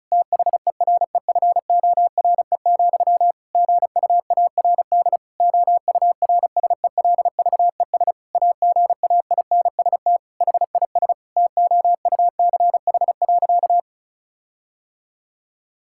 Morse code, heard as a keyed tone, about 36 wpm, 700 Hz. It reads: THEREFORE, GUARD OURSELVES AGAINST HIS TOUCH.
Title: Sentences from Top 1600 Words 35wpm